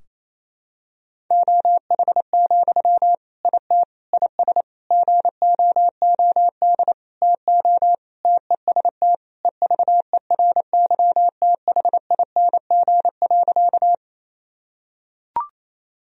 Morse code, heard as a keyed tone, about 28 wpm, 700 Hz. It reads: OH, IT IS GOOD TO TEST EVERYTHING.
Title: Sentences from Top 1000 Words - Review Part 3 - 28wpm